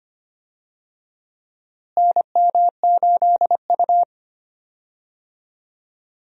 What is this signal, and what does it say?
Morse code, heard as a keyed tone, about 25 wpm, 700 Hz. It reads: NM8U
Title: USA Call Signs - 4 Characters - Repeated 2x - 25wpm